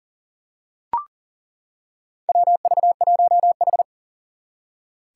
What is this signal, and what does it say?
Morse code, heard as a keyed tone, about 42 wpm, 700 Hz. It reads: WV1H
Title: Four Challenging Letter-Number Combinations 40wpm